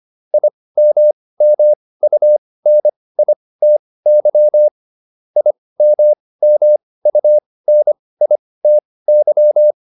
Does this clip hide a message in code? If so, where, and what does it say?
Yes: Morse code from 0.3 to 9.7 s, heard as a keyed tone, about 25 wpm, spaced as for 18 wpm, 600 Hz. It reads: IMMUNITY IMMUNITY